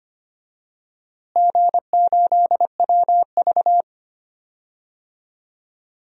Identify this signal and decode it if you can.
Morse code, heard as a keyed tone, about 25 wpm, 700 Hz. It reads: G8WV
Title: DXCC-100 Call Signs - 4 Characters - Repeated 3x - 25wpm